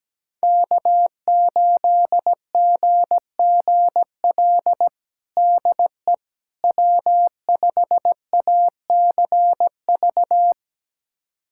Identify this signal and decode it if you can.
Morse code, heard as a keyed tone, about 17 wpm, 700 Hz. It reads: K8GGL DE W5ACV
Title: USA Callsign DE Callsign - 5 Characters - 17wpm